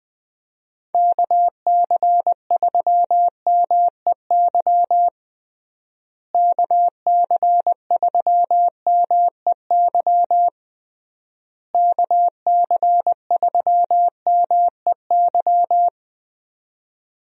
Morse code, heard as a keyed tone, about 21 wpm, 700 Hz. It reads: KC3MEY KC3MEY KC3MEY
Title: USA Call Signs - 6 Characters - Repeated 3x - 20wpm